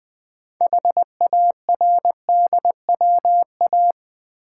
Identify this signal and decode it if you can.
Morse code, heard as a keyed tone, about 20 wpm, 700 Hz. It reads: HARDWA
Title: Build A Word - 2 Letters At A Time - 8-Letter Words - 20wpm